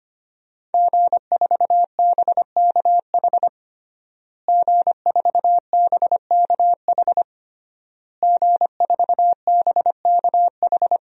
Morse code, heard as a keyed tone, about 25 wpm, 700 Hz. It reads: G4BKH G4BKH G4BKH
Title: DXCC-100 Call Signs - 5 Characters - Repeated 3x - 25wpm